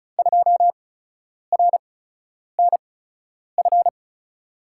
Morse code, heard as a keyed tone, about 34 wpm, spaced as for 10 wpm, 700 Hz. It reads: 2RNF
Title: USA Call Signs - 6 Characters - 8x Character Spacing - 35wpm